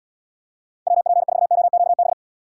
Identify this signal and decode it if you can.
Morse code, heard as a keyed tone, about 72 wpm, 700 Hz. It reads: WP4OQB